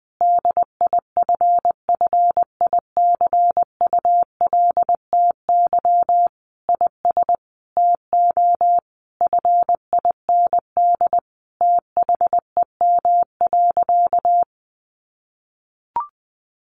Morse code, heard as a keyed tone, about 19 wpm, 700 Hz. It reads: DIFFICULTY IS TO FIND THEM.
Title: Sentences from Top 1500 Words 20wpm